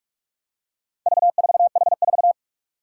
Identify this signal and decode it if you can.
Morse code, heard as a keyed tone, about 46 wpm, 700 Hz. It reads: V4H4
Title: Four Challenging Letter-Number Combinations 45wpm